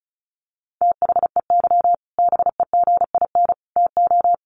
Morse code, heard as a keyed tone, about 34 wpm, 700 Hz. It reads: THEY BEGIN TO
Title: Sentences from Top 1200 Words 35wpm